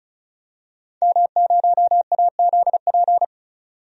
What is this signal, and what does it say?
Morse code, heard as a keyed tone, about 36 wpm, 700 Hz. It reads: M0AZP
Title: DXCC-100 Call Signs - 5 Characters - 35wpm